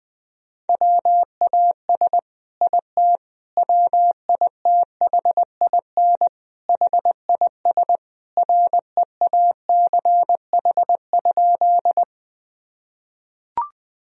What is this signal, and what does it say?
Morse code, heard as a keyed tone, about 20 wpm, 700 Hz. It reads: WAS IT WITHIN HIS REACH?
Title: Sentences from Top 1000 Words - Review Part 4 - 20wpm